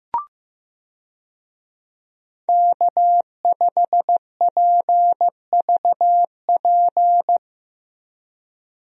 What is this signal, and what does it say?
Morse code, heard as a keyed tone, about 15 wpm, 700 Hz. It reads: K5PVP
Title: USA Call Signs - 5 Characters - Repeated 3x - 15wpm